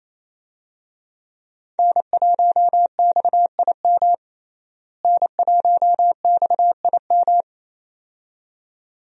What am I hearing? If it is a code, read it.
Morse code, heard as a keyed tone, about 28 wpm, 700 Hz. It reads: N1XIM N1XIM
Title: USA Call Signs - 5 Characters - Repeated 2x - 28wpm